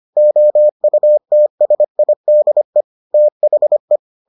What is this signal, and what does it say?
Morse code, heard as a keyed tone, about 25 wpm, 600 Hz. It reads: OUTSIDE THE